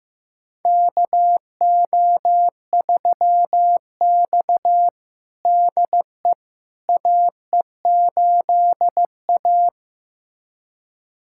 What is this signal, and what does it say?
Morse code, heard as a keyed tone, about 15 wpm, 700 Hz. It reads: KO3X DE AE8A
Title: USA Callsign DE Callsign - 4 Characters - 15wpm